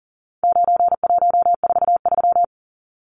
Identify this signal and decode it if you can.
Morse code, heard as a keyed tone, about 40 wpm, 700 Hz. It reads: 9143
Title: Numbers - 4-Digit 40wpm